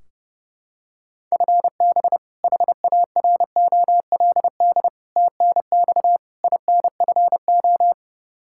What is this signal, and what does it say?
Morse code, heard as a keyed tone, about 30 wpm, 700 Hz. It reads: FB HAROLD TNX INFO
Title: Practice QSO - Exchange 3 - More Info - 30wpm